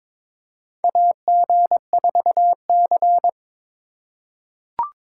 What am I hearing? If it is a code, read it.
Morse code, heard as a keyed tone, about 22 wpm, 700 Hz. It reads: AG4C
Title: USA Call Signs - 4 Characters - 4x Character Spacing - 22wpm